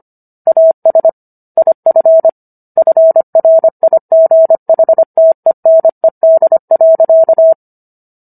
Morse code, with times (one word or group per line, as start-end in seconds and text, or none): 0.5-1.1 AS
1.6-2.3 IF
2.8-7.5 FRIGHTENED.